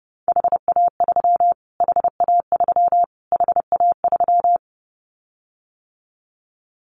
Morse code, heard as a keyed tone, about 28 wpm, 700 Hz. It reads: HA3 HA3 HA3
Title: Callsign Prefixes for Top 25 DXCC - Repeated 3x - 30wpm